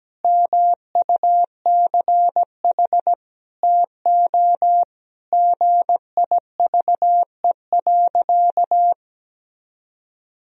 Morse code, heard as a keyed tone, about 17 wpm, 700 Hz. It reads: MUCH TO GIVE.
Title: Sentences from Top 1000 Words - Review Part 4 - 17wpm